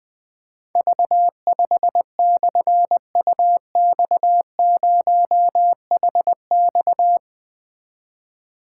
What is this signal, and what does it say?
Morse code, heard as a keyed tone, about 20 wpm, 700 Hz. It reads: V5/UX0HX
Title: Difficult Call Signs 20wpm